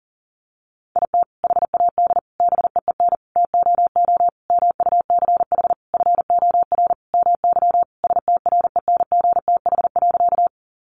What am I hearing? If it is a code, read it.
Morse code, heard as a keyed tone, about 40 wpm, 700 Hz. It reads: IT HAD BEEN TOO MUCH FOR MY STRENGTH.
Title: Sentences from Top 1300 Words 40wpm